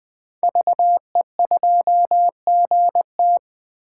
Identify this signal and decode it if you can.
Morse code, heard as a keyed tone, about 20 wpm, 700 Hz. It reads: VE2GT
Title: 2023 SOTA Chaser Call Signs - 20wpm